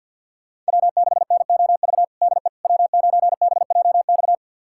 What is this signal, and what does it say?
Morse code, heard as a keyed tone, about 47 wpm, 700 Hz. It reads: W6NOV DE W9BJX
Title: USA Callsign DE Callsign - 5 Characters - 50wpm